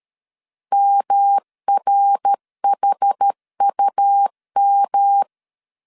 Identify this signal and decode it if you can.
Morse code, heard as a keyed tone, about 13 wpm, 800 Hz. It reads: MRHUM
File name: CW / DET / 1 6 1500C S B 12wpm 3431